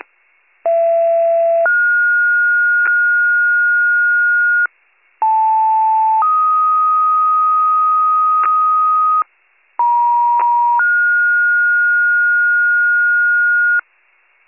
Two-tone page, heard as a one-second tone, then a three-second tone, about 670 then 1430 Hz, about 880 then 1250 Hz, about 950 then 1430 Hz, with noise. It is heard from 0.6 to 13.8 s.